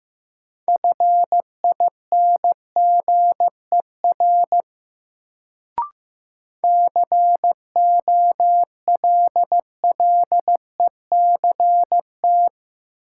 Morse code, heard as a keyed tone, about 15 wpm, 700 Hz. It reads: FINGER COLLECT
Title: Sentences from Top 1600 Words - New Word Drill Alt - 15wpm